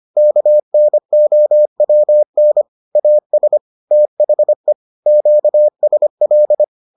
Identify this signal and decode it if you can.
Morse code, heard as a keyed tone, about 25 wpm, 600 Hz. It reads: KNOWN AS THE QSL